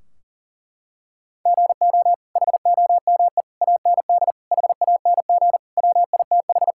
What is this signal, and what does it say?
Morse code, heard as a keyed tone, about 40 wpm, 700 Hz. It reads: GO HOME AND HANG WITH
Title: Sentences from Top 1600 Words 40wpm